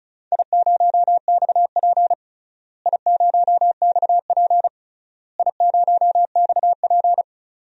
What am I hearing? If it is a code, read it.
Morse code, heard as a keyed tone, about 36 wpm, 690 Hz. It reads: I0XP I0XP I0XP
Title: DXCC-100 Call Signs - 4 Characters - Repeated 3x - 35wpm